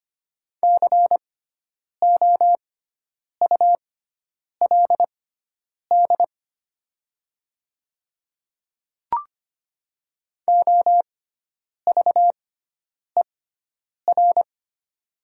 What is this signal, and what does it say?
Morse code, heard as a keyed tone, about 25 wpm, spaced as for 9 wpm, 700 Hz. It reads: COULD OVER
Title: 6x Character Spacing - Top 100 Words - 25wpm